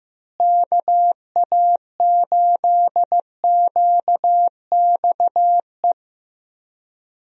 Morse code, heard as a keyed tone, about 15 wpm, 700 Hz. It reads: KA8QXE
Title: USA Call Signs - 6 Characters - Repeated 3x - 15wpm